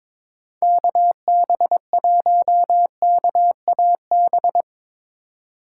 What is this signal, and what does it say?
Morse code, heard as a keyed tone, about 21 wpm, 700 Hz. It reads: KB1KAB
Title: USA Call Signs - 6 Characters - 2x Character Spacing - 22wpm